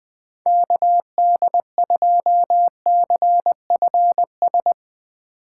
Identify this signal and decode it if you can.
Morse code, heard as a keyed tone, about 20 wpm, 700 Hz. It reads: KD2CFS